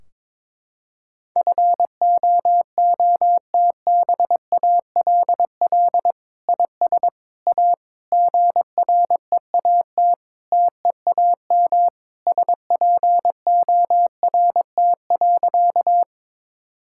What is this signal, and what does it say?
Morse code, heard as a keyed tone, about 21 wpm, 700 Hz. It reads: FOOTBALL IS A GREAT TEAM SPORT.
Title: Sentences from Top 1600 Words 22wpm